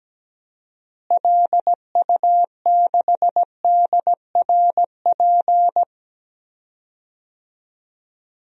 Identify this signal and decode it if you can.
Morse code, heard as a keyed tone, about 17 wpm, 700 Hz. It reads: LU6DRP
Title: DXCC-100 Call Signs - 6 Characters - Repeated 3x - 17wpm